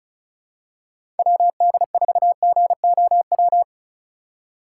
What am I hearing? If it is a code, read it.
Morse code, heard as a keyed tone, about 36 wpm, 700 Hz. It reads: WD4GOW
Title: USA Call Signs - 6 Characters - 4x Character Spacing - 35wpm